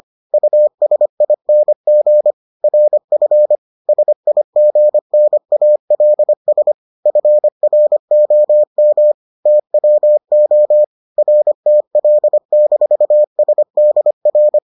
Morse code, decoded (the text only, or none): USING RF SIGNALS FROM TWO RTL-SDR